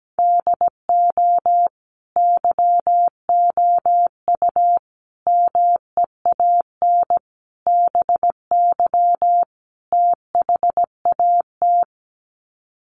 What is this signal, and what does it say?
Morse code, heard as a keyed tone, about 17 wpm, 700 Hz. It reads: DO YOU MEAN BY THAT
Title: Sets of 5 Words Encore2 17wpm